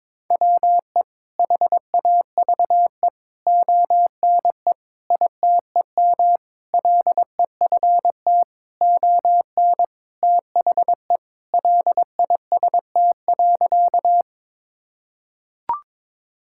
Morse code, heard as a keyed tone, about 22 wpm, 700 Hz. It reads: WE HAVE ONE ITEM LEFT ON THE LIST.
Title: Sentences from Top 1000 Words - Review Part 6 - 22wpm